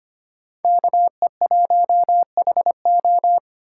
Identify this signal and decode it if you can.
Morse code, heard as a keyed tone, about 25 wpm, 700 Hz. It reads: KE1HO
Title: USA Call Signs - 5 Characters - 3x Character Spacing - 25wpm